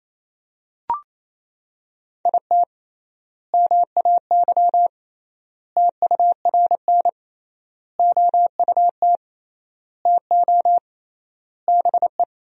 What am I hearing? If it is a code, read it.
Morse code, heard as a keyed tone, about 28 wpm, 700 Hz. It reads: IT MAY TURN OUT TO BE
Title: 3x Word-Spacing - Sentences from Top 400 Words - 28wpm